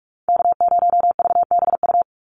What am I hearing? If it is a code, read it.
Morse code, heard as a keyed tone, about 46 wpm, 700 Hz. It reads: K0VBU
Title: Call Signs - 45wpm